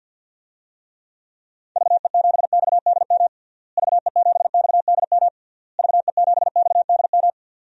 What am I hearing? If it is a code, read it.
Morse code, heard as a keyed tone, about 52 wpm, 700 Hz. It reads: VE7XDM VE7XDM VE7XDM
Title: DXCC-100 Call Signs - 6 Characters - Repeated 3x - 50wpm